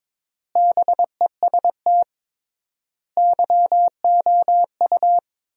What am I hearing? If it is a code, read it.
Morse code, heard as a keyed tone, about 22 wpm, 700 Hz. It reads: BEST YOU
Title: 3x Word-Spacing - Sentences from Top 500 Words - 22wpm